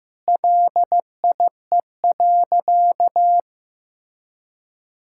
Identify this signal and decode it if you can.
Morse code, heard as a keyed tone, about 15 wpm, 700 Hz. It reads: LIE.
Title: Sentences from Top 1600 Words 15wpm